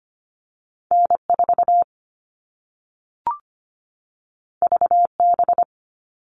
Morse code, heard as a keyed tone, about 25 wpm, 700 Hz. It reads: N4 VB
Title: All Cut Numbers - 2-Digit 25wpm